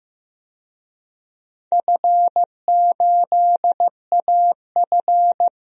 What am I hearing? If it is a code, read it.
Morse code, heard as a keyed tone, about 15 wpm, 700 Hz. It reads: F8AF
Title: DXCC-100 Call Signs - 4 Characters - 15wpm